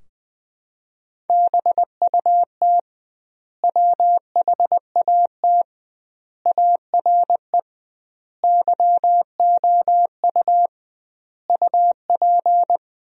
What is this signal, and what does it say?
Morse code, heard as a keyed tone, about 20 wpm, 700 Hz. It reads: BUT WHAT ARE YOU UP
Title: 2x Word-Spacing - Sentences from Top 100 Words - 20wpm